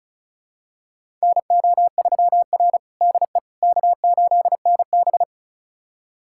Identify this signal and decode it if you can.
Morse code, heard as a keyed tone, about 34 wpm, 700 Hz. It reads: NO3R DE K8NB
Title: USA Callsign DE Callsign - 4 Characters - 35wpm